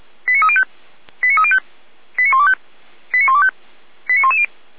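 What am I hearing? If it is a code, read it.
Five-tone selective call (ZVEI), heard as quick runs of five tones, tones of about 70 ms. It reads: 89305 89305 89125 89125 89110
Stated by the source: ZVEI five-tone call 89042 received by SDR.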